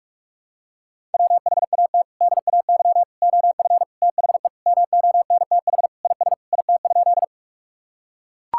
Morse code, heard as a keyed tone, about 42 wpm, 700 Hz. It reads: WHAT DAY OF THE MONTH IS IT?
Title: Sentences from Top 1000 Words - Review Part 5 - 45wpm